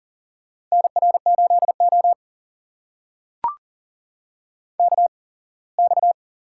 Morse code, heard as a keyed tone, about 43 wpm, spaced as for 24 wpm, 700 Hz. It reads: NR8O KX
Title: USA Call Signs - 4 Characters - 8x Character Spacing - 40wpm